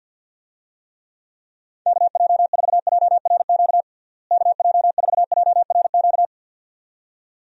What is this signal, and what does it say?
Morse code, heard as a keyed tone, about 53 wpm, 700 Hz. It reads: KJ4JRQ KJ4JRQ
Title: USA Call Signs - 6 Characters - Repeated 2x - 50wpm